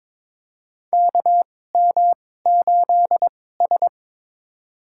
Morse code, heard as a keyed tone, about 22 wpm, spaced as for 16 wpm, 700 Hz. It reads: KM8S
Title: USA Call Signs - 4 Characters - 2x Character Spacing - 22wpm